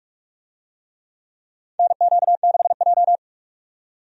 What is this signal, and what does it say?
Morse code, heard as a keyed tone, about 48 wpm, 700 Hz. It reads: NQ6J
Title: USA Call Signs - 4 Characters - 45wpm